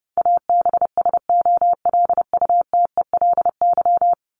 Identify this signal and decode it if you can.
Morse code, heard as a keyed tone, about 30 wpm, 700 Hz. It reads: ABSOLUTELY
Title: LY Suffix - Top 100 Words with Base Words 30wpm